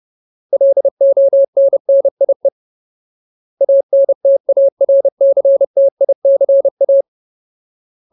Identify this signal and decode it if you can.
Morse code, heard as a keyed tone, about 30 wpm, 550 Hz. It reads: LONNIE ANTARCTICA
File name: Ultimate Word List / 30WPM Ultimate